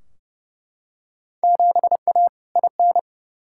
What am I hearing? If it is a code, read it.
Morse code, heard as a keyed tone, about 31 wpm, 700 Hz. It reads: 7A IN
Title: Field Day Exchanges Encore - 30wpm